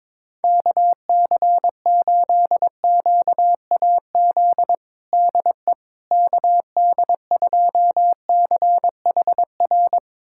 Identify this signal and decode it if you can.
Morse code, heard as a keyed tone, about 21 wpm, 700 Hz. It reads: KC8QAZ DE KD2CHR
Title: USA Callsign DE Callsign - 6 Characters - 22wpm